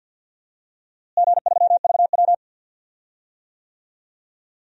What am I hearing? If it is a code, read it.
Morse code, heard as a keyed tone, about 53 wpm, 700 Hz. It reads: G3VW
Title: DXCC-100 Call Signs - 4 Characters - Repeated 3x - 50wpm